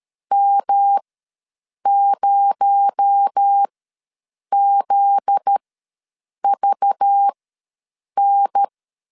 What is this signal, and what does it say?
Morse code, heard as a keyed tone, about 13 wpm, spaced as for 7 wpm, 800 Hz. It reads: M0ZVN
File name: LEZIONI / 7 GF9Z / 7 GF9Z ws 12wpm 3321